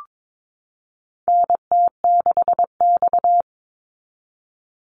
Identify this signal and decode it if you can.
Morse code, heard as a keyed tone, about 22 wpm, 700 Hz. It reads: NT6X
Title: Call Signs Repeated 3x - 22wpm